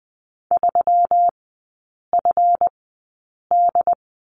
Morse code, heard as a keyed tone, about 20 wpm, spaced as for 8 wpm, 700 Hz. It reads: 3FD
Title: 138 Lesson 23e - Letter f - Three Characters 20wpm